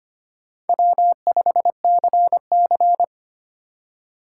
Morse code, heard as a keyed tone, about 25 wpm, 700 Hz. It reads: W5CC
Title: USA Call Signs - 4 Characters - 3x Character Spacing - 25wpm